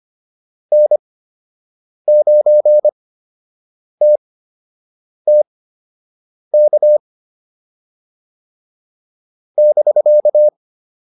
Morse code, heard as a keyed tone, about 25 wpm, spaced as for 7 wpm, 600 Hz. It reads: N9TTK <BK>
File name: QSO 205 07